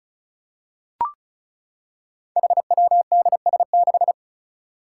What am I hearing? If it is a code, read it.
Morse code, heard as a keyed tone, about 34 wpm, 700 Hz. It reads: HWDS6
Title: Five Challenging Letter-Number Combinations 35wpm